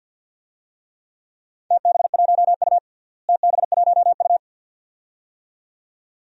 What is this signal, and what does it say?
Morse code, heard as a keyed tone, about 51 wpm, 700 Hz. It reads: TB1U TB1U
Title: DXCC-100 Call Signs - 4 Characters - Repeated 2x - 50wpm